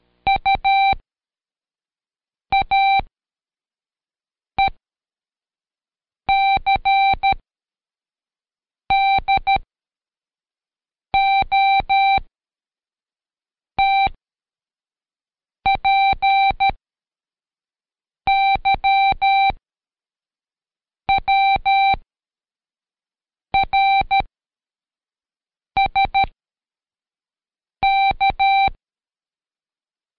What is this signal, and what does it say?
Morse code, heard as a keyed tone, about 13 wpm, spaced as for 5 wpm, 780 Hz. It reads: UAECDOTPYWRSK
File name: Code Quick / AudioDownld2 / Seg95-Day11-FinalReview 2